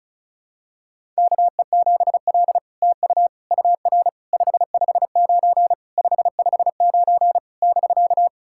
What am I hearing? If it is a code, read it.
Morse code, heard as a keyed tone, about 35 wpm, 700 Hz. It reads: KE7L TU UR 559 559 <BK>
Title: SOTA QSO Practice 35wpm